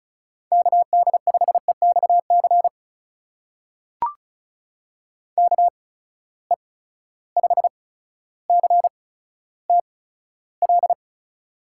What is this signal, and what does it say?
Morse code, heard as a keyed tone, about 36 wpm, spaced as for 15 wpm, 700 Hz. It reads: KD5EXC KE5CTL
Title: USA Call Signs - 6 Characters - 8x Character Spacing - 35wpm